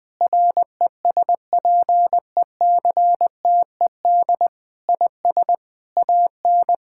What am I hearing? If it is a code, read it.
Morse code, heard as a keyed tone, about 19 wpm, 700 Hz. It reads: RESPECTED IS AN
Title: Sentences with 6-Word Phrases - 20wpm